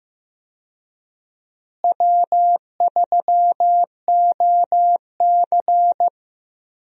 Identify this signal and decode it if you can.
Morse code, heard as a keyed tone, about 16 wpm, 700 Hz. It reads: W3OC
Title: USA Call Signs - 4 Characters - Repeated 2x - 15wpm